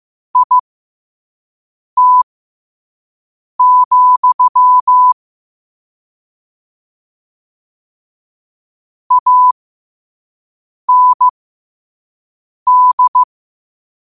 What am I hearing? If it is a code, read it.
Morse code, heard as a keyed tone, about 15 wpm, spaced as for 5 wpm, 1000 Hz. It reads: IT, AND
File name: qotdfiles / 5wpm